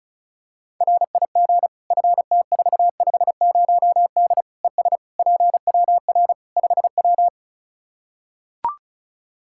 Morse code, heard as a keyed tone, about 36 wpm, 700 Hz. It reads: RIG FT450D ES PWR 5W
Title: Practice QSO - Exchange 2 - Info - 35wpm